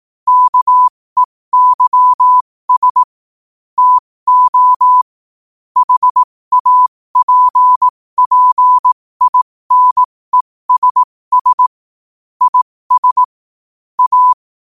Morse code, heard as a keyed tone, about 18 wpm, spaced as for 15 wpm, 1000 Hz. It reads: KEYS TO HAPPINESS IS A
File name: qotdfiles / 15wpm